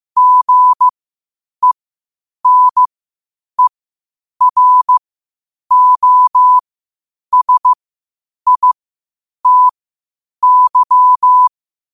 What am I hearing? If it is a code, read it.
Morse code, heard as a keyed tone, about 15 wpm, spaced as for 8 wpm, 1000 Hz. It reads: GENEROSITY
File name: qotdfiles / 8wpm